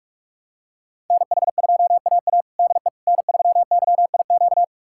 Morse code, heard as a keyed tone, about 46 wpm, 700 Hz. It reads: NH2AA DE N3YIQ
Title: USA Callsign DE Callsign - 5 Characters - 45wpm